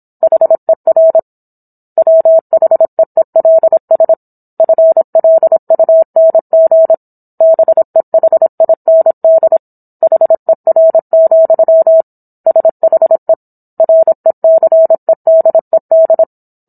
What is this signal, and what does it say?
Morse code, heard as a keyed tone, about 27 wpm, 650 Hz. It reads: HER WHEELS FLUNG BEHIND HER, SHE RECEDED
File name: Book One / 17-Chapter 17 - 26 WPM